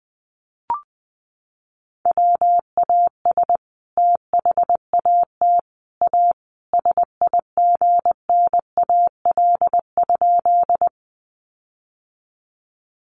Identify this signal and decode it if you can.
Morse code, heard as a keyed tone, about 20 wpm, 700 Hz. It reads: WAS THAT A SIGNAL?
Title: Sentences from Top 1400 Words 20wpm